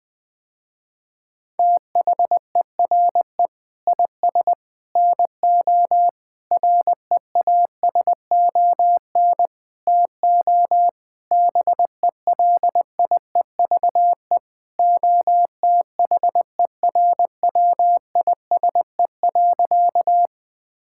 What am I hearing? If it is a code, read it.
Morse code, heard as a keyed tone, about 20 wpm, 700 Hz. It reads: THERE IS NO REASON TO BELIEVE OTHERWISE.
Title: Sentences with 6-Word Phrases - Spoken Beginning - 20wpm